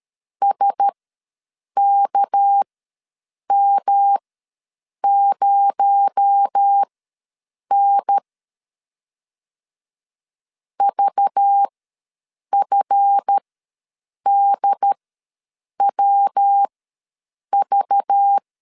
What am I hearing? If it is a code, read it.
Morse code, heard as a keyed tone, about 13 wpm, spaced as for 7 wpm, 800 Hz. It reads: SKM0N VFDWV
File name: LEZIONI / 7 GF9Z / 7 GF ws 12wpm 3140